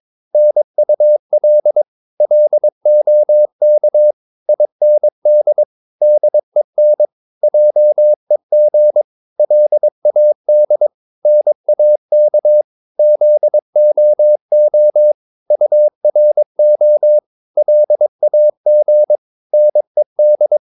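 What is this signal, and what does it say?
Morse code, heard as a keyed tone, about 22 wpm, 600 Hz. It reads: NUL LOK IND DEN JEG LAD NAK ZOO URO LAG NED
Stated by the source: Morse practice of short Danish words at 22 wpm, from CW med Gnister.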